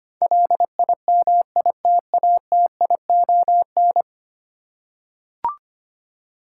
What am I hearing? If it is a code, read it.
Morse code, heard as a keyed tone, about 25 wpm, 700 Hz. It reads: LIMITATION
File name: Ten Letter Words - 25wpm